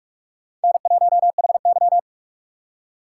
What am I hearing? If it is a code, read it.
Morse code, heard as a keyed tone, about 44 wpm, 700 Hz. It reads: N1HY